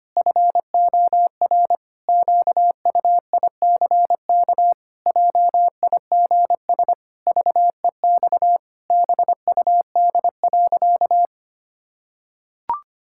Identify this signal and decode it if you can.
Morse code, heard as a keyed tone, about 25 wpm, 700 Hz. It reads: FOR QUICK JIGS VEX BUD.
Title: Pangram Challenge v2 - 25wpm